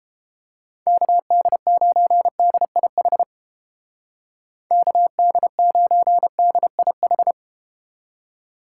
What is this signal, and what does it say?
Morse code, heard as a keyed tone, about 30 wpm, 700 Hz. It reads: KD9DIH KD9DIH